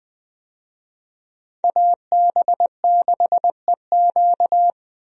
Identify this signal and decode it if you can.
Morse code, heard as a keyed tone, about 21 wpm, 700 Hz. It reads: AB6EQ
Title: USA Call Signs - 5 Characters - Repeated 3x - 20wpm